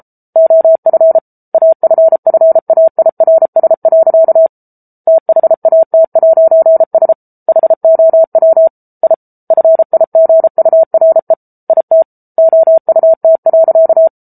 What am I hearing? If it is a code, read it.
Morse code, heard as a keyed tone, about 34 wpm, 650 Hz. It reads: OF AFFAIRS. THAT'S HOW I FIGURE IT OUT.